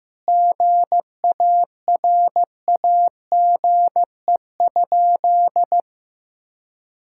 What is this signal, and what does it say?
Morse code, heard as a keyed tone, about 16 wpm, 700 Hz. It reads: GARAGE?
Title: Sentences from Top 1600 Words 15wpm